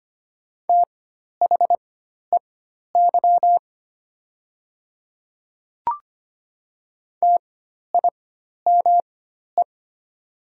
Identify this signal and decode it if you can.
Morse code, heard as a keyed tone, about 25 wpm, spaced as for 12 wpm, 700 Hz. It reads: THEY TIME
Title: 4x Character Spacing - Top 500 Words - 25wpm